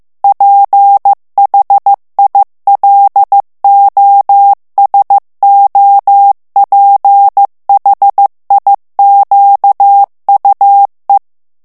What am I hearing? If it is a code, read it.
Morse code, heard as a keyed tone, about 15 wpm, 800 Hz. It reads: PHILOSOPHIQUE
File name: morse mp3 / FRANCAIS TEXTE CLAIR 2 VITESSE 14 MOTS MINUTES